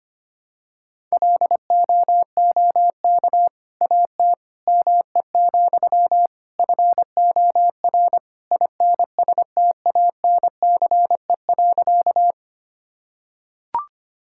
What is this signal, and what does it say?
Morse code, heard as a keyed tone, about 25 wpm, 700 Hz. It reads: LOOK AT ME, FOR INSTANCE.
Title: Sentences from Top 1000 Words - Review Part 5 - 25wpm